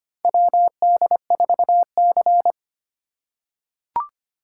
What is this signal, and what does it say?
Morse code, heard as a keyed tone, about 25 wpm, 700 Hz. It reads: WD4C
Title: USA Call Signs - 4 Characters - 3x Character Spacing - 25wpm